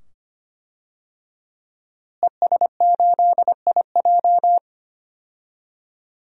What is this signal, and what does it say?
Morse code, heard as a keyed tone, about 25 wpm, 700 Hz. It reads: ES8IJ
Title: DXCC-100 Call Signs - 5 Characters - Repeated 2x - 25wpm